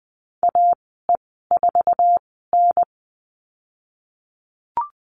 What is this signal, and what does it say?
Morse code, heard as a keyed tone, about 19 wpm, spaced as for 14 wpm, 700 Hz. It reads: AE4N
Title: 048 Lesson 9g - Number 4 - Call Signs 20wpm